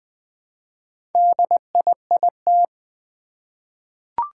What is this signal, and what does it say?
Morse code, heard as a keyed tone, about 20 wpm, 700 Hz. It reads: DIIT
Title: Four Challenging Letter-Number Combinations 20wpm